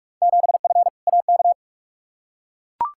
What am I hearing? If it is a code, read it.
Morse code, heard as a keyed tone, about 43 wpm, 700 Hz. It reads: 7F AK
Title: Field Day Exchanges Encore - 45wpm